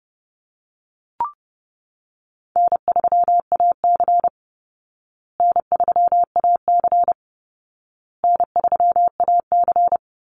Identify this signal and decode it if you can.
Morse code, heard as a keyed tone, about 30 wpm, 700 Hz. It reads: N3AC N3AC N3AC
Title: Call Signs Repeated 2x - 30wpm